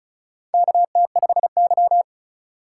Morse code, heard as a keyed tone, about 36 wpm, 700 Hz. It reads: KT5Y
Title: USA Call Signs - 4 Characters - 6x Character Spacing - 35wpm